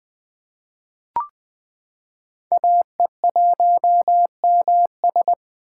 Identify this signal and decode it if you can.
Morse code, heard as a keyed tone, about 20 wpm, 700 Hz. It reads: AE1MS
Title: 2023 SOTA Chaser Call Signs - 20wpm